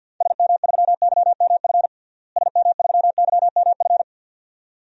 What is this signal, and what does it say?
Morse code, heard as a keyed tone, about 46 wpm, 700 Hz. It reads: SM3YMF SM3YMF
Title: DXCC-100 Call Signs - 6 Characters - Repeated 2x - 50wpm